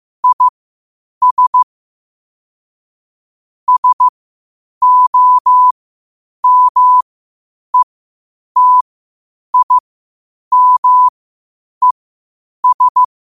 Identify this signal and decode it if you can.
Morse code, heard as a keyed tone, about 15 wpm, spaced as for 8 wpm, 1000 Hz. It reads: IS SOMETIMES
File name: qotdfiles / 8wpm